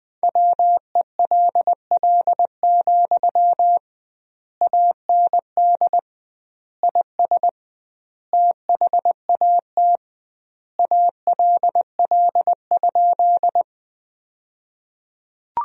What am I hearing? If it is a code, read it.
Morse code, heard as a keyed tone, about 20 wpm, 700 Hz. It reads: WELL, AND IS THAT ALL?
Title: 2x Word-Spacing - Sentences from Top 200 Words - 20wpm